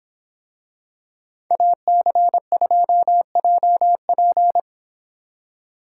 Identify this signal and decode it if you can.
Morse code, heard as a keyed tone, about 25 wpm, 700 Hz. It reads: AC2JP